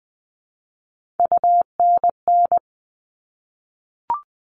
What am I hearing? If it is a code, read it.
Morse code, heard as a keyed tone, about 20 wpm, 700 Hz. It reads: UNN
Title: All Cut Numbers - 3-Digit 20wpm